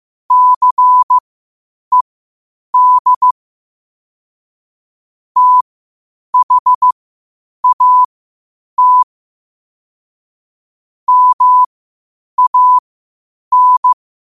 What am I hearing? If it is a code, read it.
Morse code, heard as a keyed tone, about 15 wpm, spaced as for 8 wpm, 1000 Hz. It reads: CED THAT MAN
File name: qotdfiles / 8wpm